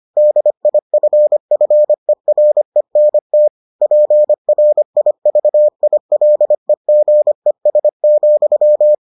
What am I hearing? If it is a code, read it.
Morse code, heard as a keyed tone, about 25 wpm, 600 Hz. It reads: DIFFERENT PRIVILEGES,